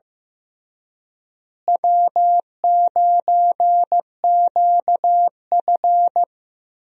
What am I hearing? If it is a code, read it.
Morse code, heard as a keyed tone, about 15 wpm, 700 Hz. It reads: W9QF
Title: USA Call Signs - 4 Characters - Repeated 3x - 15wpm